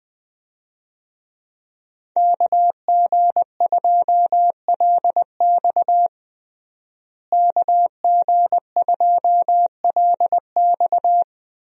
Morse code, heard as a keyed tone, about 21 wpm, 700 Hz. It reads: KG2LX KG2LX
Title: USA Call Signs - 5 Characters - Repeated 3x - 20wpm